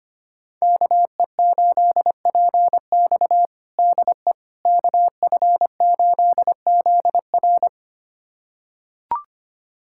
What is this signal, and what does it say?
Morse code, heard as a keyed tone, about 25 wpm, 700 Hz. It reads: KE8PX DE KF8ZR
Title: USA Callsign DE Callsign - 5 Characters - 25wpm